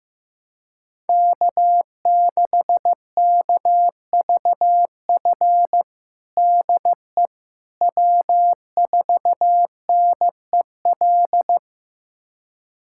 Morse code, heard as a keyed tone, about 15 wpm, 700 Hz. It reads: K6KVF DE W4NEL
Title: USA Callsign DE Callsign - 5 Characters - 15wpm